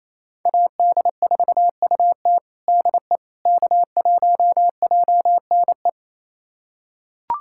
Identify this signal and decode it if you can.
Morse code, heard as a keyed tone, about 27 wpm, 700 Hz. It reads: AD4UT DE K1JNE
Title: USA Callsign DE Callsign - 5 Characters - 28wpm